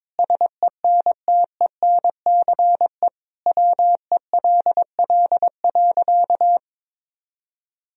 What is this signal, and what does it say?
Morse code, heard as a keyed tone, about 22 wpm, 700 Hz. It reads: SENTENCE WELL.
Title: Sentences from Top 1000 Words - Review Part 1 - 22wpm